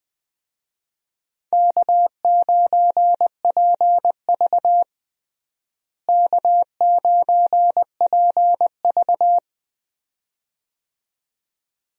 Morse code, heard as a keyed tone, about 20 wpm, 700 Hz. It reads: K9PV K9PV
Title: USA Call Signs - 4 Characters - Repeated 2x - 20wpm